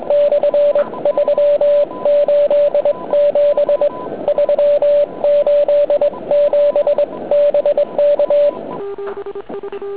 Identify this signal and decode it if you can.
Morse code, heard as a keyed tone, about 22 wpm, 580 Hz. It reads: / 3 8 7 3 8 7 BK